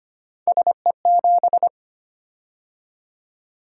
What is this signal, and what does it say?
Morse code, heard as a keyed tone, about 25 wpm, 700 Hz. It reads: SE7
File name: Three Letter-Number Combinations - 25wpm